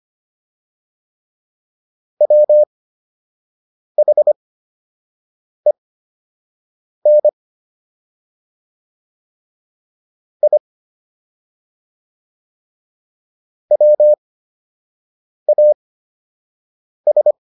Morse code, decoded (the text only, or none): WHEN I WAS